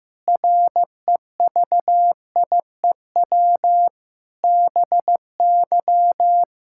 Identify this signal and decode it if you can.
Morse code, heard as a keyed tone, about 15 wpm, 700 Hz. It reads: REVIEW BY
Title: Sentences from Top 1000 Words - Review Part 6 - 15wpm